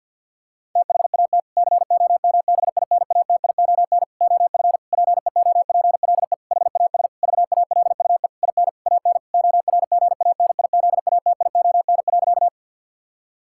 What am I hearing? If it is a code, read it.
Morse code, heard as a keyed tone, about 50 wpm, 700 Hz. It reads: THAT COMBINATION OF PEOPLE HAS VALUE IN AN ORGANIZATION.
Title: Sentences from Top 1000 Words - Review Part 6 - 50wpm